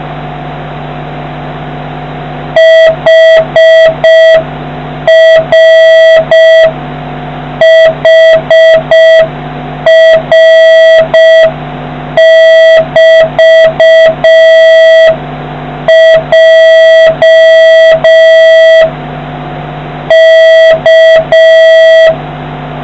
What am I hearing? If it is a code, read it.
Morse code, heard as a keyed tone, about 5 wpm, 1950 Hz, with noise. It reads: HRHR<BT>J K